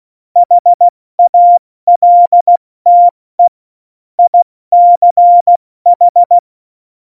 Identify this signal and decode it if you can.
Morse code, heard as a keyed tone, about 16 wpm, 700 Hz. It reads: HALTE ICH
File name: Morsekurs / Witze / Witz 13 March 2018, 70 F80